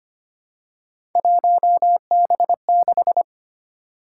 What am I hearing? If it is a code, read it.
Morse code, heard as a keyed tone, about 25 wpm, 700 Hz. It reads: 1B6
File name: Three Letter-Number Combinations - 25wpm